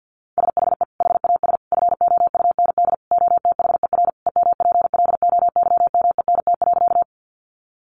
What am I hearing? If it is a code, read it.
Morse code, heard as a keyed tone, about 50 wpm, 700 Hz. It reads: SHE HAS FOUND OTHER EMPLOYMENT.
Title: Sentences from Top 1500 Words 50wpm